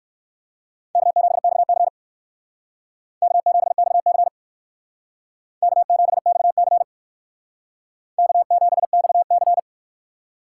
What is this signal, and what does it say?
Morse code, heard as a keyed tone, about 57 wpm, 700 Hz. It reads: K7XC K7XC K7XC K7XC